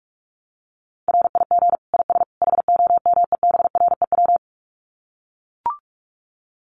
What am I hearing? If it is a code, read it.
Morse code, heard as a keyed tone, about 44 wpm, 700 Hz. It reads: RIG IS HOMEBREW
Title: Common QSO Phrases 45wpm